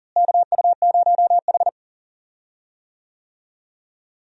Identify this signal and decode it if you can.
Morse code, heard as a keyed tone, about 40 wpm, 700 Hz. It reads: KU0H
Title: USA Call Signs - 4 Characters - Repeated 3x - 40wpm